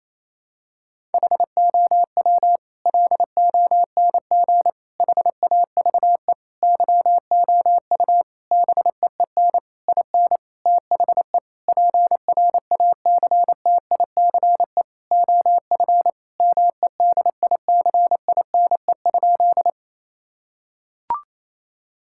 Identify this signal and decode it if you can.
Morse code, heard as a keyed tone, about 28 wpm, 700 Hz. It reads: HOW LONG HAVE YOU BEEN IN THE PRACTICE OF MEDICINE?
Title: Sentences from Top 1000 Words - Review Part 6 - 28wpm